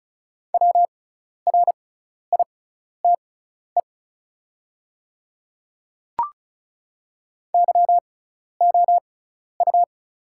Morse code, heard as a keyed tone, about 36 wpm, spaced as for 12 wpm, 700 Hz. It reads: WRITE YOU
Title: 6x Character Spacing - Top 100 Words - 35wpm